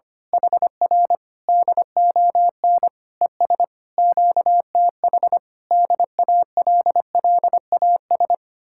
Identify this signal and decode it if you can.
Morse code, heard as a keyed tone, about 24 wpm, 700 Hz. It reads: HR DON ES QTH DALLAS
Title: Speed-Racing Common QSO Phrases - 25wpm